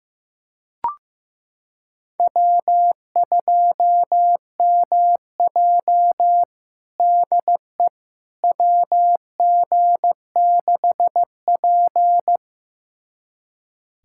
Morse code, heard as a keyed tone, about 15 wpm, 700 Hz. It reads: W2MJ DE WG6P
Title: USA Callsign DE Callsign - 4 Characters - 15wpm